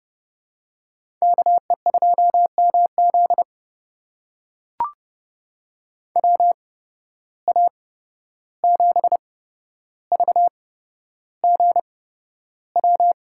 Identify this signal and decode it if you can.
Morse code, heard as a keyed tone, about 32 wpm, spaced as for 12 wpm, 700 Hz. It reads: KE2MZ WA7VGW
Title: USA Call Signs - 8x Character Spacing - 30wpm